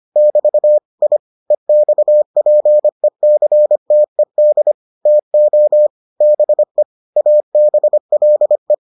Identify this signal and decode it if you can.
Morse code, heard as a keyed tone, about 25 wpm, 600 Hz. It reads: <BT> I EXPECTED TO BE ABLE